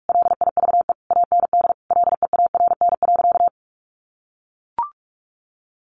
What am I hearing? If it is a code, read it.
Morse code, heard as a keyed tone, about 44 wpm, 700 Hz. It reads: LIVE AND LEARN.
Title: Sentences from Top 800 Words 45wpm